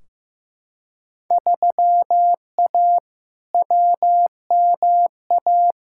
Morse code, heard as a keyed tone, about 15 wpm, 700 Hz. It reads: 3A WMA
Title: Field Day Exchanges Encore - 15wpm